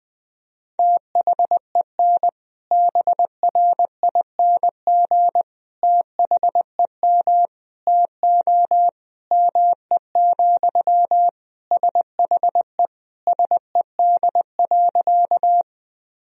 Morse code, heard as a keyed tone, about 21 wpm, 700 Hz. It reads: THEN BRING THEM TO ME, SHE SED.
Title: Sentences from Top 500 Words and Abbreviations 20wpm